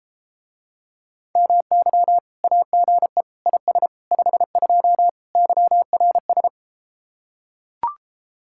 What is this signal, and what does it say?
Morse code, heard as a keyed tone, about 33 wpm, 700 Hz. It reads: MY AGE IS 52 YRS